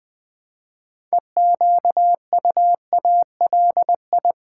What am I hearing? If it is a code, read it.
Morse code, heard as a keyed tone, about 20 wpm, 700 Hz. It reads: EQUALI
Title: Build A Word - 3 Letters At A Time - 8-Letter Words - 20wpm